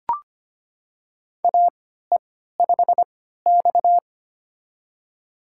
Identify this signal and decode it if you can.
Morse code, heard as a keyed tone, about 25 wpm, spaced as for 14 wpm, 700 Hz. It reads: AE5X
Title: USA Call Signs - 4 Characters - 3x Character Spacing - 25wpm